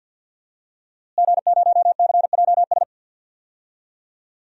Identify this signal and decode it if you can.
Morse code, heard as a keyed tone, about 51 wpm, 700 Hz. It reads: G0CJS
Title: DXCC-100 Call Signs - 5 Characters - Repeated 2x - 50wpm